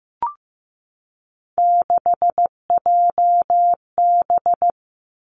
Morse code, heard as a keyed tone, about 15 wpm, 700 Hz. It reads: 6JB
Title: Three Challenging Letter-Number Combinations 15wpm